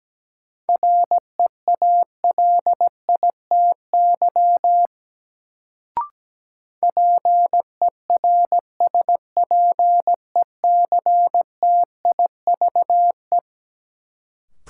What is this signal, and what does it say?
Morse code, heard as a keyed tone, about 17 wpm, 700 Hz. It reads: REALITY PERSPECTIVE
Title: Sentences from Top 1200 Words - New Word Drill Alt - 17wpm